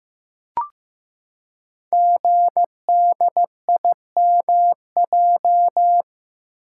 Morse code, heard as a keyed tone, about 15 wpm, 700 Hz. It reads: GDIMJ
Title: Code Groups - 5 Letters - 15wpm